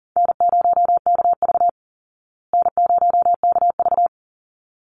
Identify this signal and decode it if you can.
Morse code, heard as a keyed tone, about 37 wpm, 700 Hz. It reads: N0KV N0KV
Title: Call Signs Repeated 3x - 40wpm